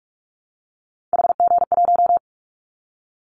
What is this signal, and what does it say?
Morse code, heard as a keyed tone, about 46 wpm, 700 Hz. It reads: HG1
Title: Callsign Prefixes for Top 50 DXCC - 45wpm